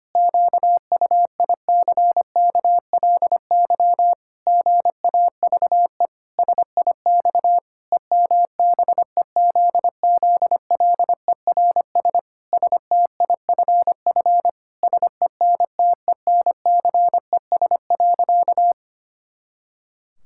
Morse code, heard as a keyed tone, about 25 wpm, 700 Hz. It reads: QUICKLY GAVE SIX EMBEZZLERS STIFF SENTENCES.
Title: Pangram Challenge v2 - 25wpm